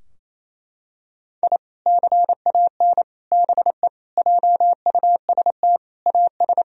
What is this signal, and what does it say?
Morse code, heard as a keyed tone, about 28 wpm, 700 Hz. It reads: I CAN BE JUST AS
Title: Sentences from Top 1000 Words - Review Part 3 - 28wpm